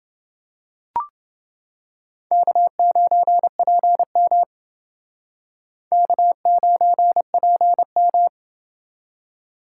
Morse code, heard as a keyed tone, about 29 wpm, 700 Hz. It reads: K9PM K9PM